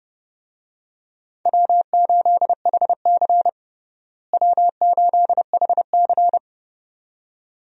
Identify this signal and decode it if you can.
Morse code, heard as a keyed tone, about 30 wpm, 700 Hz. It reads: W8HC W8HC
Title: USA Call Signs - 4 Characters - Repeated 2x - 30wpm